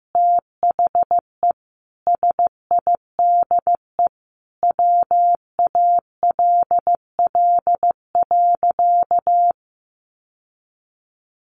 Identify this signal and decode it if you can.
Morse code, heard as a keyed tone, about 15 wpm, 700 Hz. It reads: THE SIDE WALL.
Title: Sentences from Top 1100 Words 15wpm